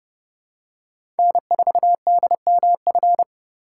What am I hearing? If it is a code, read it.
Morse code, heard as a keyed tone, about 30 wpm, 700 Hz. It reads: N4DMF
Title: USA Call Signs - 5 Characters - 3x Character Spacing - 30wpm